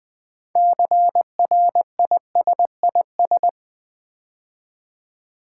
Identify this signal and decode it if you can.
Morse code, heard as a keyed tone, about 20 wpm, 700 Hz. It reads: CRISIS
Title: 6 Letter Words - First 3 Letters Given - 20wpm